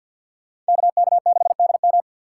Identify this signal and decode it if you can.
Morse code, heard as a keyed tone, about 51 wpm, 690 Hz. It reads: KK6DM
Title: USA Call Signs - 5 Characters - 4x Character Spacing - 50wpm